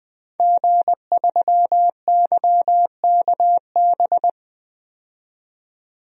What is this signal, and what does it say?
Morse code, heard as a keyed tone, about 21 wpm, 700 Hz. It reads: G3YKB
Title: DXCC-100 Call Signs - 5 Characters - 20wpm